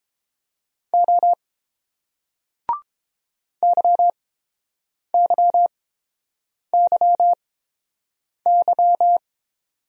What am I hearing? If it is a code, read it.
Morse code, heard as a keyed tone, about 30 wpm, spaced as for 7 wpm, 700 Hz. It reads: O YYYY